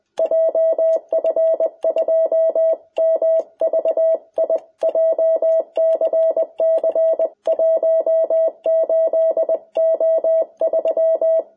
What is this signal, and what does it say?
Morse code, heard as a keyed tone, about 20 wpm, 660 Hz. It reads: JF2MVI JCC 18O3
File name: jh8cbh / 2021cw